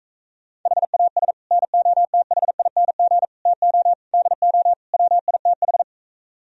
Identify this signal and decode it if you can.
Morse code, heard as a keyed tone, about 45 wpm, 700 Hz. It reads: HAS NOTHING TO DO WITH